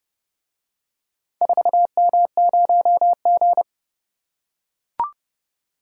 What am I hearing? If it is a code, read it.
Morse code, heard as a keyed tone, about 31 wpm, 700 Hz. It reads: 4M0G